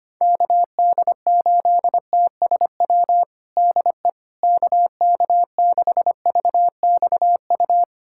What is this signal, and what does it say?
Morse code, heard as a keyed tone, about 25 wpm, 700 Hz. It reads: KD8TSW DE KK6VXU
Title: USA Callsign DE Callsign - 6 Characters - 25wpm